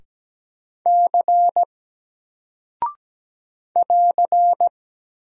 Morse code, heard as a keyed tone, about 17 wpm, 700 Hz. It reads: C <AR>
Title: Single Character (Letter:Number:Punctuation:Prosign) Alt 17wpm